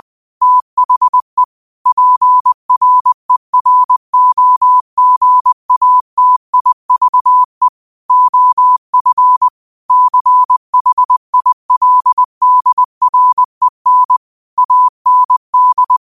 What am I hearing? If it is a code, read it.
Morse code, heard as a keyed tone, about 20 wpm, 1000 Hz. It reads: THE PREROGATIVE OF CHILDREN AND
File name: qotdfiles / 20wpm